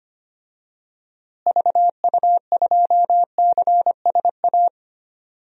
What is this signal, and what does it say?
Morse code, heard as a keyed tone, about 25 wpm, 700 Hz. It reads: VU2CSA
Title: DXCC-100 Call Signs - 6 Characters - Repeated 2x - 25wpm